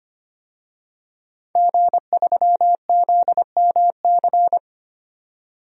Morse code, heard as a keyed tone, about 25 wpm, 700 Hz. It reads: G3ZMC
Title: DXCC-100 Call Signs - 5 Characters - 25wpm